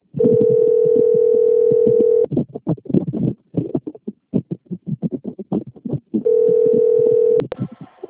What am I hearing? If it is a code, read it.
Ringback tone (North America), 440 + 480 Hz.
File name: out- - - - - .17889